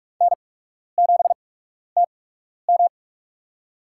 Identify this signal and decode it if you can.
Morse code, heard as a keyed tone, about 46 wpm, spaced as for 12 wpm, 690 Hz. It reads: N7TM
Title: USA Call Signs - 4 Characters - 8x Character Spacing - 45wpm